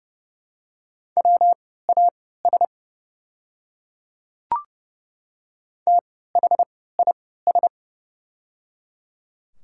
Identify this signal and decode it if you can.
Morse code, heard as a keyed tone, about 30 wpm, spaced as for 17 wpm, 690 Hz. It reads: WAS THIS
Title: 3x Character Spacing - Top 300 Words - 30wpm